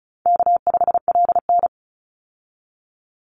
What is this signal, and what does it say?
Morse code, heard as a keyed tone, about 34 wpm, 700 Hz. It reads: K5LN
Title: Easy Call Signs Repeated 3x - 35wpm